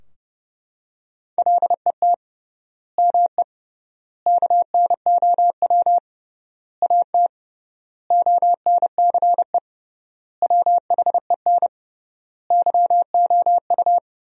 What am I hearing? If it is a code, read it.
Morse code, heard as a keyed tone, about 30 wpm, 700 Hz. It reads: LET ME KNOW AT ONCE WHEN YOU
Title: 3x Word-Spacing - Sentences from Top 300 Words - 30wpm